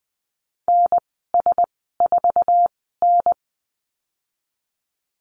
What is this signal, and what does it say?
Morse code, heard as a keyed tone, about 20 wpm, spaced as for 14 wpm, 700 Hz. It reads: NS4N
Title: 048 Lesson 9g - Number 4 - Call Signs 20wpm